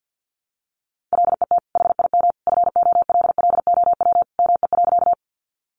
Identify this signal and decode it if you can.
Morse code, heard as a keyed tone, about 50 wpm, 700 Hz. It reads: LET HIM FOLLOW ME.
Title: Sentences from Top 300 Words 50wpm